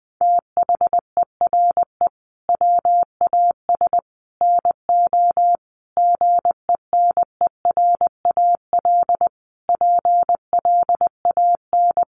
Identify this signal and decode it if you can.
Morse code, heard as a keyed tone, about 20 wpm, 700 Hz. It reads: THERE WAS NO GENERAL PLAN
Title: Sentences from Top 1200 Words 20wpm